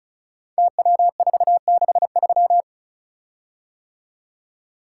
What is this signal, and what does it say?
Morse code, heard as a keyed tone, about 36 wpm, 700 Hz. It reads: TW463
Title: Code Groups - 5 Letter:Numbers - 35wpm